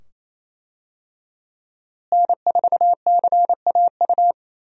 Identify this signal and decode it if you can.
Morse code, heard as a keyed tone, about 28 wpm, 700 Hz. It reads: N4CAU
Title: USA Call Signs - 5 Characters - Repeated 2x - 28wpm